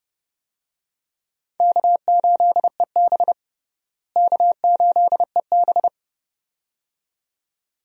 Morse code, heard as a keyed tone, about 30 wpm, 700 Hz. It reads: K8EB K8EB
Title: USA Call Signs - 4 Characters - Repeated 2x - 30wpm